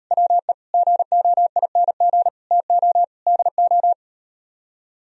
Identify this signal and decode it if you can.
Morse code, heard as a keyed tone, about 38 wpm, 700 Hz. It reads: WE GOING TO DO